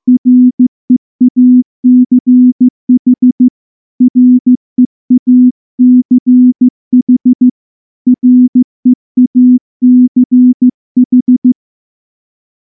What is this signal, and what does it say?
Morse code, heard as a keyed tone, about 14 wpm, 250 Hz. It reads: REACH REACH REACH
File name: morse-training-15WPM